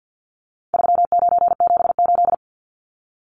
Morse code, heard as a keyed tone, about 50 wpm, 700 Hz. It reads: 3978
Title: Numbers - 4-Digit 50wpm